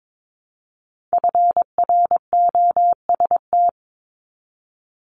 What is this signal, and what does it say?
Morse code, heard as a keyed tone, about 22 wpm, 700 Hz. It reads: FROST
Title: Multi-letter Phonemes Repeated 3x 22wpm